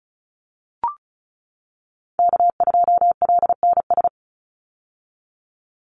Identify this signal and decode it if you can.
Morse code, heard as a keyed tone, about 34 wpm, 700 Hz. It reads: K2LNS
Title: Call Signs - 35wpm